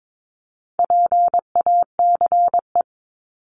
Morse code, heard as a keyed tone, about 22 wpm, 700 Hz. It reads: PACE
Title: Top 1751-2000 Words 22wpm